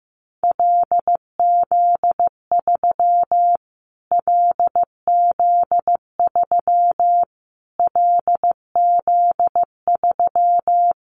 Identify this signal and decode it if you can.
Morse code, heard as a keyed tone, about 15 wpm, 700 Hz. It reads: LZ3 LZ3 LZ3
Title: Callsign Prefixes for Top 25 DXCC - Repeated 3x - 15wpm